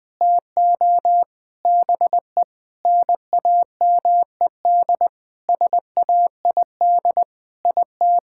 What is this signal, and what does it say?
Morse code, heard as a keyed tone, about 21 wpm, 700 Hz. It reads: TO BE NAMED SAID IT
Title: Sentences with 7-Word Phrases - Spoken Beginning - 20wpm